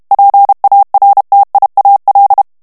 Morse code, heard as a keyed tone, about 32 wpm, 800 Hz. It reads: PARTIAL
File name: ANGLAIS texte clair 3 vitesse 30 mots minute